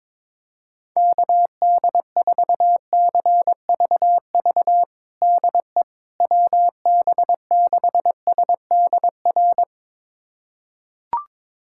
Morse code, heard as a keyed tone, about 21 wpm, 700 Hz. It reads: KD4CVV DE WB6SDR
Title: USA Callsign DE Callsign - 6 Characters - 22wpm